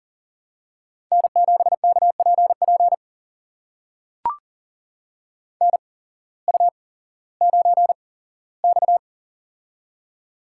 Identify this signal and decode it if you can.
Morse code, heard as a keyed tone, about 40 wpm, spaced as for 19 wpm, 700 Hz. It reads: N7KPP NU9X
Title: USA Call Signs - 8x Character Spacing - 40wpm